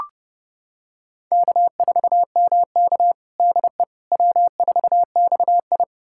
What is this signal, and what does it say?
Morse code, heard as a keyed tone, about 30 wpm, 700 Hz. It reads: K4MK DE W4XI
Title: USA Callsign DE Callsign - 4 Characters - 30wpm